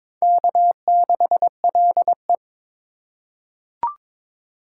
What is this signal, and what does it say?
Morse code, heard as a keyed tone, about 22 wpm, 700 Hz. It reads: K6LE
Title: USA Call Signs - 4 Characters - 2x Character Spacing - 22wpm